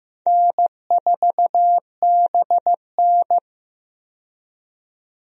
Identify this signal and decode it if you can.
Morse code, heard as a keyed tone, about 15 wpm, 700 Hz. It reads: N4BN
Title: USA Call Signs - 4 Characters - 15wpm